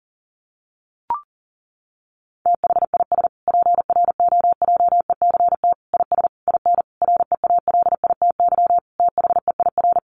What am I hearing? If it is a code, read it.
Morse code, heard as a keyed tone, about 40 wpm, 700 Hz. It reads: THIS PROJECT IS IN REALITY THEIR
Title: Sentences from Top 1400 Words 40wpm